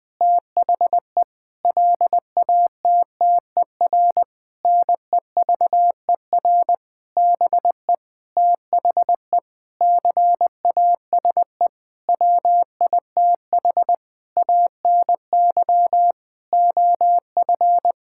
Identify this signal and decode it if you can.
Morse code, heard as a keyed tone, about 20 wpm, 700 Hz. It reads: THE LATTER NEVER BE THE CASE WITH ANY OF
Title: Sentences from Top 1600 Words 20wpm